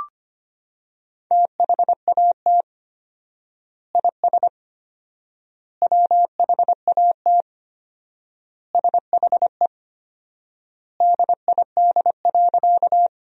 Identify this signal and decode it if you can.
Morse code, heard as a keyed tone, about 25 wpm, 700 Hz. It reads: THAT IS WHAT SHE DID.
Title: 4x Word-Spacing - Sentences from Top 100 Words - 25wpm